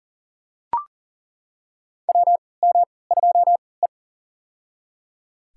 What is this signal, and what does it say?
Morse code, heard as a keyed tone, about 37 wpm, spaced as for 22 wpm, 700 Hz. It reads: WM2E
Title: USA Call Signs - 4 Characters - 3x Character Spacing - 40wpm